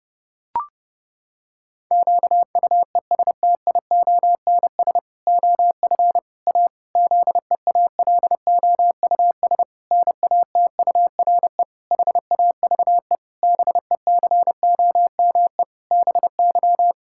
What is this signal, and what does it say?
Morse code, heard as a keyed tone, about 30 wpm, 700 Hz. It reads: QUESTIONS OF A ZEALOUS NATURE HAVE BECOME BY
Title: Pangram Challenge v2 - 30wpm